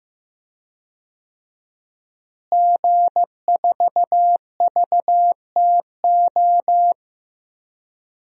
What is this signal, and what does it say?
Morse code, heard as a keyed tone, about 15 wpm, 700 Hz. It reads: G4VTO
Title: DXCC-100 Call Signs - 5 Characters - Repeated 3x - 15wpm